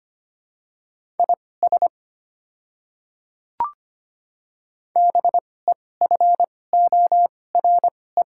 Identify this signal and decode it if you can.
Morse code, heard as a keyed tone, about 25 wpm, spaced as for 18 wpm, 700 Hz. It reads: IS BEFORE
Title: 2x Character Spacing - Top 200 Words - 25wpm